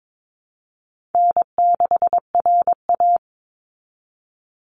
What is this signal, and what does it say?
Morse code, heard as a keyed tone, about 22 wpm, 700 Hz. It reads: N6RA
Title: Easy Call Signs Repeated 3x - 22wpm